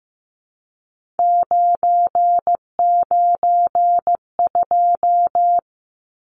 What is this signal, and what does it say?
Morse code, heard as a keyed tone, about 15 wpm, 700 Hz. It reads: 992
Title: Numbers - 3-Digit 15wpm